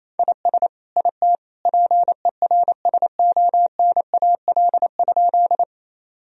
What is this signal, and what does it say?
Morse code, heard as a keyed tone, about 27 wpm, 700 Hz. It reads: IS IT PERSONAL?
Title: Sentences from Top 1000 Words - Review Part 4 - 28wpm